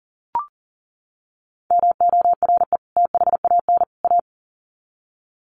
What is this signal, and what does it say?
Morse code, heard as a keyed tone, about 40 wpm, 700 Hz. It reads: MORE THAN A
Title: Sets of 3 Words 40wpm